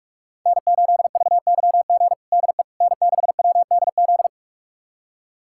Morse code, heard as a keyed tone, about 42 wpm, 700 Hz. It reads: N8VYG DE N6WDZ
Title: USA Callsign DE Callsign - 5 Characters - 45wpm